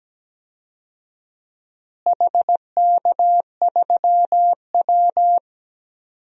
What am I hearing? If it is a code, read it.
Morse code, heard as a keyed tone, about 17 wpm, 700 Hz. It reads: HK3W
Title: DXCC-100 Call Signs - 4 Characters - Repeated 3x - 17wpm